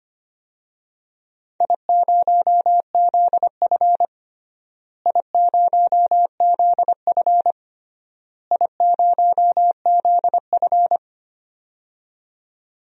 Morse code, heard as a keyed tone, about 25 wpm, 700 Hz. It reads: I0ZF I0ZF I0ZF
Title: DXCC-100 Call Signs - 4 Characters - Repeated 3x - 25wpm